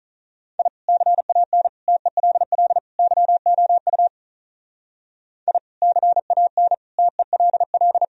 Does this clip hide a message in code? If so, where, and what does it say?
Yes: Morse code from 0.6 to 8.0 s, heard as a keyed tone, about 44 wpm, 700 Hz. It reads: I CAN TELL YOU I CAN TELL